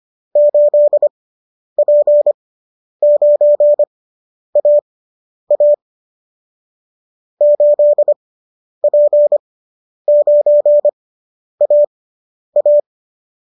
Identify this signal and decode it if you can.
Morse code, heard as a keyed tone, about 25 wpm, spaced as for 10 wpm, 590 Hz. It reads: 8P9AA 8P9AA